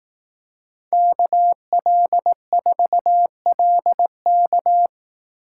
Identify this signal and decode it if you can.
Morse code, heard as a keyed tone, about 18 wpm, 700 Hz. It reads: KL4LK
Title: Speed Racing - USA Call Signs 15wpm